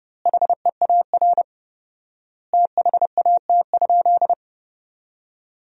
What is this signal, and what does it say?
Morse code, heard as a keyed tone, about 32 wpm, 700 Hz. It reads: HEAR THAT?
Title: 4x Word-Spacing - Sentences from Top 300 Words - 30wpm